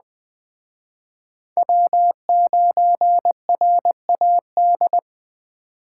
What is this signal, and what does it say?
Morse code, heard as a keyed tone, about 20 wpm, 700 Hz. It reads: W9RAD